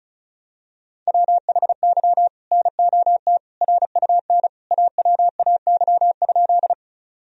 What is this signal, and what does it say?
Morse code, heard as a keyed tone, about 34 wpm, 700 Hz. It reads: WHY NOT RUN AWAY?
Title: Sentences from Top 1000 Words - Review Part 1 - 35wpm